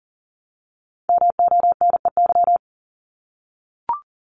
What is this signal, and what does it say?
Morse code, heard as a keyed tone, about 38 wpm, 700 Hz. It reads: MONEY
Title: Multi-letter Phonemes Repeated 3x 40wpm